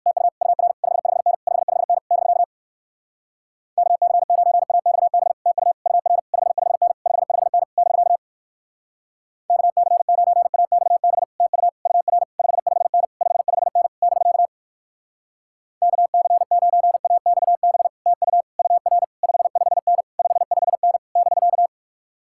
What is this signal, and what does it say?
Morse code, heard as a keyed tone, about 59 wpm, 700 Hz. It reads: TU UR 55N 55N <BK> KC9AXB TU UR 55N 55N <BK> KC9AXB TU UR 55N 55N <BK> KC9AXB TU UR 55N 55N <BK>